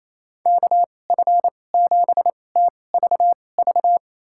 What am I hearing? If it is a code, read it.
Morse code, heard as a keyed tone, about 28 wpm, spaced as for 20 wpm, 700 Hz. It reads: KF7TVV
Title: USA Call Signs - 6 Characters - 2x Character Spacing - 28wpm